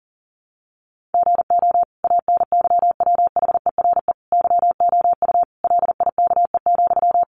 Morse code, heard as a keyed tone, about 40 wpm, 700 Hz. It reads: GO ANYWHERE YOU LIKE,
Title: Sentences from Top 1400 Words 40wpm